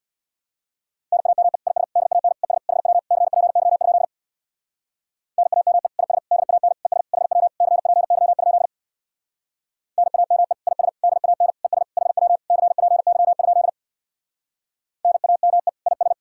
Morse code, heard as a keyed tone, about 64 wpm, 700 Hz. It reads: NAME IS DAN ES HW CPY? NAME IS DAN ES HW CPY? NAME IS DAN ES HW CPY? NAME IS